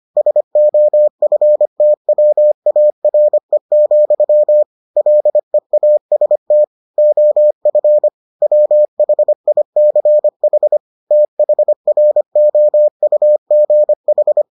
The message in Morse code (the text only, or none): SOFTWARE, LEAST OF WHICH THROUGH